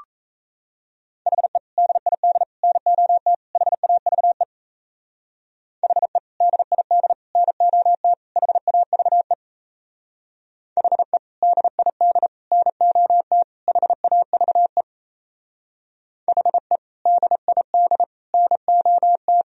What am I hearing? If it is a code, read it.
Morse code, heard as a keyed tone, about 35 wpm, 700 Hz. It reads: HE DID NOT HAVE HE DID NOT HAVE HE DID NOT HAVE HE DID NOT
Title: Speed-Racing - Sets of 4 Words 28wpm